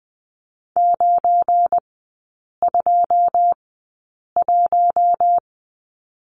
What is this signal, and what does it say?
Morse code, heard as a keyed tone, about 20 wpm, spaced as for 8 wpm, 700 Hz. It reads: 921
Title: 205 Lesson 33e - Letter j - Three Characters 20wpm